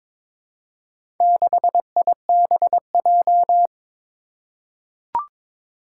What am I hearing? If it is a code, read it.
Morse code, heard as a keyed tone, about 21 wpm, 700 Hz. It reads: 6IBJ
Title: Four Challenging Letter-Number Combinations 22wpm